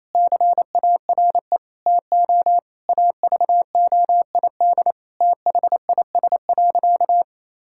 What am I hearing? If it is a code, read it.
Morse code, heard as a keyed tone, about 28 wpm, 700 Hz. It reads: CARE TO AVOID THIS.
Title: Sentences from Top 1000 Words - Review Part 6 - 28wpm